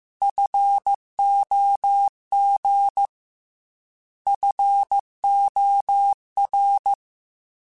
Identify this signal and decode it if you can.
Morse code, heard as a keyed tone, about 15 wpm, 800 Hz. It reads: FOG FOR
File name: Alphabetical / Alphabetical mp3s / Fs @13